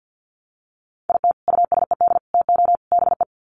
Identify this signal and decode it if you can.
Morse code, heard as a keyed tone, about 50 wpm, 700 Hz. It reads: IT USED TO BE
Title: Sets of 4 Words 50wpm